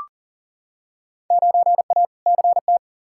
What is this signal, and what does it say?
Morse code, heard as a keyed tone, about 40 wpm, 700 Hz. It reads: 9A CT
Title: Field Day Exchanges Encore - 40wpm